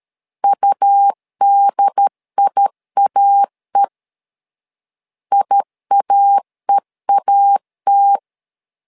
Morse code, heard as a keyed tone, about 13 wpm, 800 Hz. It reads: UDIAE IAEAT
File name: LEZIONI / 2 UHDN / 2 UHDN 12wpm 1456